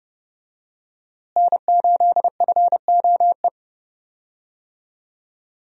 Morse code, heard as a keyed tone, about 31 wpm, 700 Hz. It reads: N8FOE
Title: USA Call Signs - 5 Characters - Repeated 2x - 30wpm